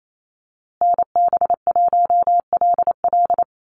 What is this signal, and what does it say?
Morse code, heard as a keyed tone, about 28 wpm, 700 Hz. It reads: NB1LL
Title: USA Call Signs Repeated 3x 28wpm